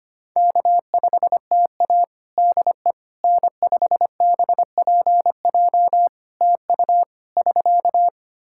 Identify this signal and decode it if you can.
Morse code, heard as a keyed tone, about 25 wpm, 700 Hz. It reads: K5TA DE N5BPJ TU <SK>
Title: K1USN SST QSO Practice 25wpm